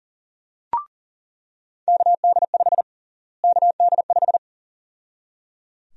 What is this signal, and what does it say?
Morse code, heard as a keyed tone, about 38 wpm, 700 Hz. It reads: KD5 KD5
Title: USA Call Sign Prefixes - Repeated 2x - 40wpm